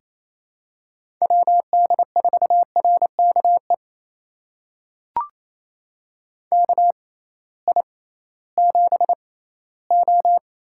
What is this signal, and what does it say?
Morse code, heard as a keyed tone, about 28 wpm, spaced as for 16 wpm, 700 Hz. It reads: WD4RKE KI7O
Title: USA Call Signs - 6 Characters - 6x Character Spacing - 28wpm